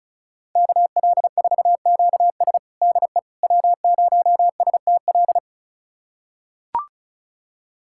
Morse code, heard as a keyed tone, about 36 wpm, 700 Hz. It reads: KL4QS DE W0STL
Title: USA Callsign DE Callsign - 5 Characters - 35wpm